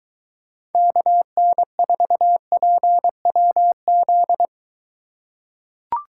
Morse code, heard as a keyed tone, about 23 wpm, 700 Hz. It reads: KN4PWZ